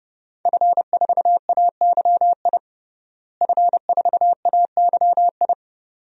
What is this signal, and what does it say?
Morse code, heard as a keyed tone, about 30 wpm, 700 Hz. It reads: F4AYI F4AYI
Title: DXCC-100 Call Signs - 5 Characters - Repeated 2x - 30wpm